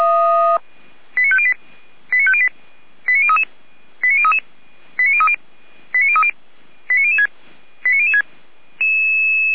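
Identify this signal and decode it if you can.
Five-tone selective call (ZVEI), heard as quick runs of five tones, tones of about 70 ms. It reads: 89508 89508 89033 89033 89030 89030 89006 89006
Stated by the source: ZVEI five-tone selective call 89011 received with an SDR.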